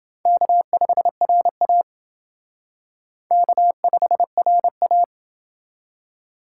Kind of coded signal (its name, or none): Morse code